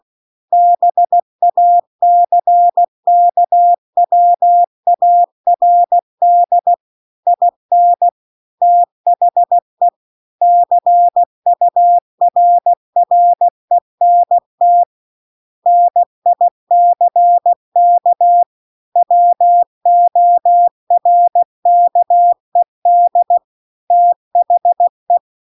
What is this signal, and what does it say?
Morse code, heard as a keyed tone, about 16 wpm, 700 Hz. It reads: BACKWARD IN THE CURRENT NICK WORKED THE
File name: CW-The-Big-Twoheaded-River-2-16WPM